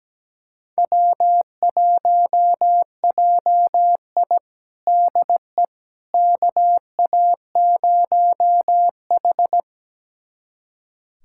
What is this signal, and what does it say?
Morse code, heard as a keyed tone, about 17 wpm, 700 Hz. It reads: W1JI DE KA0H
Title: USA Callsign DE Callsign - 4 Characters - 17wpm